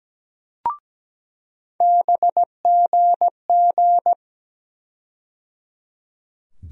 Morse code, heard as a keyed tone, about 17 wpm, 700 Hz. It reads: BGG
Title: USA Call Sign Postfixes - 3 Letters 17wpm